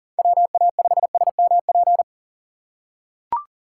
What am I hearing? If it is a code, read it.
Morse code, heard as a keyed tone, about 40 wpm, 700 Hz. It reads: WA5SMP
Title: USA Call Signs - 6 Characters - 6x Character Spacing - 40wpm